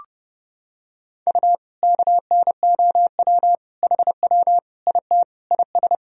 Morse code, heard as a keyed tone, about 30 wpm, 700 Hz. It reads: U KNOW HW IT IS
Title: Sentences from Top 100 Words and Abbreviations 30wpm